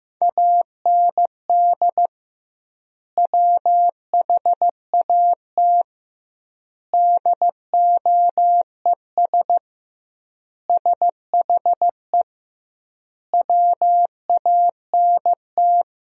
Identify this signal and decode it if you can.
Morse code, heard as a keyed tone, about 15 wpm, 700 Hz. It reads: AND WHAT DOES SHE WANT
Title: 2x Word-Spacing - Sentences from Top 400 Words - 15wpm